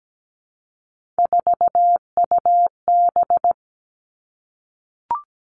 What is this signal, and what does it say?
Morse code, heard as a keyed tone, about 17 wpm, 700 Hz. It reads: 4UB
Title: Three Challenging Letter-Number Combinations 17wpm